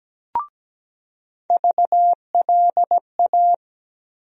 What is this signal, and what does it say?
Morse code, heard as a keyed tone, about 17 wpm, 700 Hz. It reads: VLA
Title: USA Call Sign Postfixes - 3 Letters 17wpm